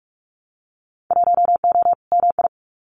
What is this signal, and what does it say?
Morse code, heard as a keyed tone, about 44 wpm, 700 Hz. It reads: 1O MI
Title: Winter Field Day Exchanges 45wpm